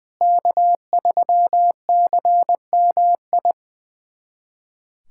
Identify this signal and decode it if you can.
Morse code, heard as a keyed tone, about 21 wpm, 700 Hz. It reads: K3CMI
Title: USA Callsign DE Callsign 20wpm